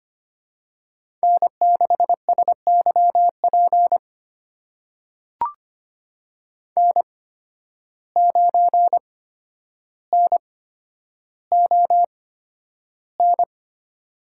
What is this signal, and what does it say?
Morse code, heard as a keyed tone, about 25 wpm, spaced as for 11 wpm, 700 Hz. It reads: N6SYP N9NON
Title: USA Call Signs - 5 Characters - 8x Character Spacing - 25wpm